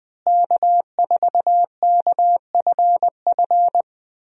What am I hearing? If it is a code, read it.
Morse code, heard as a keyed tone, about 20 wpm, 700 Hz. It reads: K4KFF
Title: USA Call Signs - 5 Characters - Repeated 3x - 20wpm